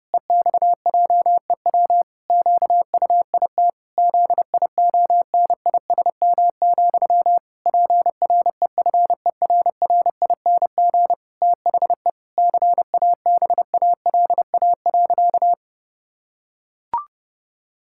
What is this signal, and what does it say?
Morse code, heard as a keyed tone, about 30 wpm, 700 Hz. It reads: EXJEW QUIT ZIONISM, PREFERRING THE CABALA.
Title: Pangram Challenge v2 - 30wpm